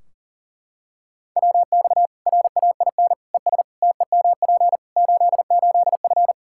Morse code, heard as a keyed tone, about 37 wpm, 700 Hz. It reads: WX RAIN ES TEMP 88F
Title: Practice QSO - Exchange 2 - Info - 40wpm